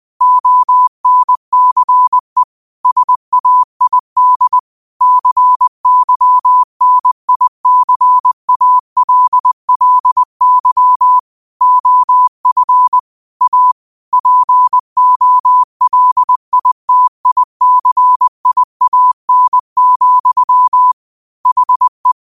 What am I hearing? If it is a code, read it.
Morse code, heard as a keyed tone, about 20 wpm, 1000 Hz. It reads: ONCE SAID CYNICALLY OF A POLITICIAN, HE
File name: qotdfiles / 20wpm